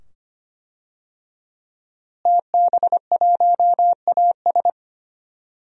Morse code, heard as a keyed tone, about 25 wpm, 700 Hz. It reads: TB1AS
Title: DXCC-100 Call Signs - 5 Characters - Repeated 2x - 25wpm